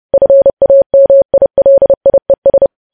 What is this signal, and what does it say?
Morse code, heard as a keyed tone, about 30 wpm, 550 Hz. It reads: FAMILIES
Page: news headlines in morse code at 30 wpm